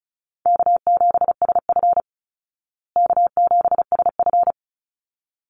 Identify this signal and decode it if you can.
Morse code, heard as a keyed tone, about 34 wpm, 700 Hz. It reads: K7SF K7SF
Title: Easy Call Signs Repeated 2x - 35wpm